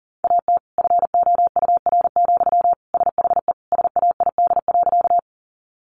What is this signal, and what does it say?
Morse code, heard as a keyed tone, about 37 wpm, 700 Hz. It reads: AT FOUR, SHE SAID.
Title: Sentences from Top 300 Words 40wpm